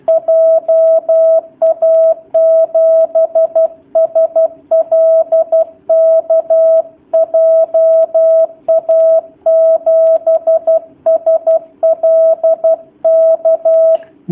Morse code, heard as a keyed tone, about 12 wpm, 650 Hz. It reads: JA7SLK JA7SLK
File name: jh8cbh / 2019cw / sp11- w